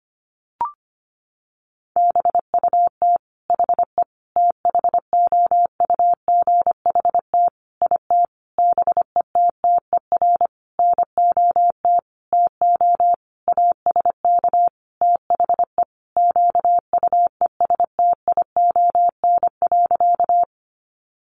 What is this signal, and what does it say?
Morse code, heard as a keyed tone, about 25 wpm, 700 Hz. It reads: BUT HE THOUGHT IT BETTER NOT TO ASK THE QUESTION.
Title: Sentences from Top 600 Words 25wpm